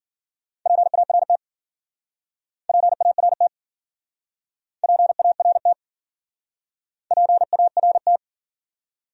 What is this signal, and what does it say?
Morse code, heard as a keyed tone, about 51 wpm, 700 Hz. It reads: PART PART PART PART